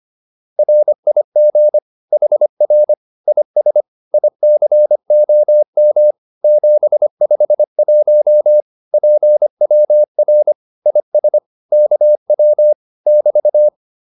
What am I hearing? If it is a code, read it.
Morse code, heard as a keyed tone, about 25 wpm, 600 Hz. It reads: RIG HR IS ICOM 751 PWR IS KW <BT>
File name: QSO210a 25